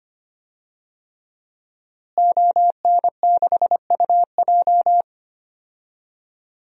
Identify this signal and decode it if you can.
Morse code, heard as a keyed tone, about 25 wpm, 700 Hz. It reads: ON6UJ
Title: DXCC-100 Call Signs - 5 Characters - Repeated 3x - 25wpm